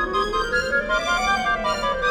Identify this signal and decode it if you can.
There is no code.